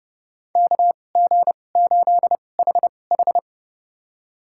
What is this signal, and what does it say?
Morse code, heard as a keyed tone, about 30 wpm, spaced as for 22 wpm, 700 Hz. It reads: KG8HH
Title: USA Call Signs - 2x Character Spacing - 30wpm